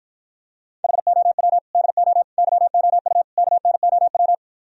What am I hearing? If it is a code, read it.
Morse code, heard as a keyed tone, about 52 wpm, 700 Hz. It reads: HOW DO YOU KNOW